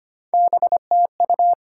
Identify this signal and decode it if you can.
Morse code, heard as a keyed tone, about 35 wpm, spaced as for 30 wpm, 700 Hz. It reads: BTU
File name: practice / 25wpm / Speed-Racing Top 20 QSO Elements - 25wpm